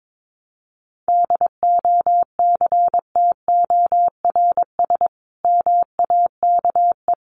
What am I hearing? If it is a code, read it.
Morse code, heard as a keyed tone, about 22 wpm, 700 Hz. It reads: DOCTORS MAKE
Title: Easy Phrases 22wpm